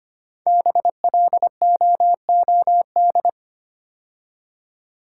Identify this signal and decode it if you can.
Morse code, heard as a keyed tone, about 25 wpm, 700 Hz. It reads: BLOOD
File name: 5 Letter Words - First 3 Letters Given - 25wpm